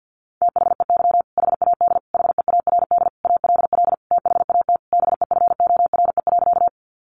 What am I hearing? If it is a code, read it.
Morse code, heard as a keyed tone, about 50 wpm, 700 Hz. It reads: THEY HAD HEARD ALL THAT BEFORE.
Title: Sentences from Top 400 Words 50wpm